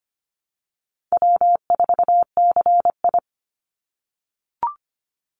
Morse code, heard as a keyed tone, about 25 wpm, 700 Hz. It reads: W4CI
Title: Call Signs - 25wpm